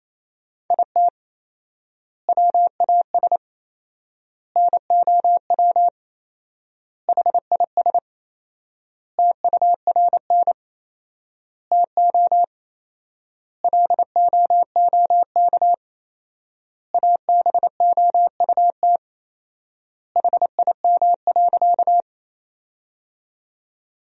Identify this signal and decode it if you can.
Morse code, heard as a keyed tone, about 27 wpm, 700 Hz. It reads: IT WAS NOW HIS TURN TO LOOK ABOUT HIM.
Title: 4x Word-Spacing - Sentences from Top 200 Words - 28wpm